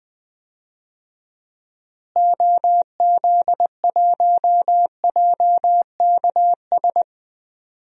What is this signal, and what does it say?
Morse code, heard as a keyed tone, about 21 wpm, 700 Hz. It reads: OZ1JKS
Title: DXCC-100 Call Signs - 6 Characters - Repeated 2x - 20wpm